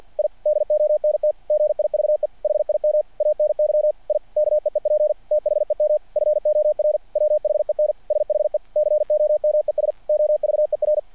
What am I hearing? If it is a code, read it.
Morse code, heard as a keyed tone, about 50 wpm, 600 Hz. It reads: I DONT GIVE HIM ANY I KEEP THEM FOR WHEN SHE COMES OVER
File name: archived / CRSnet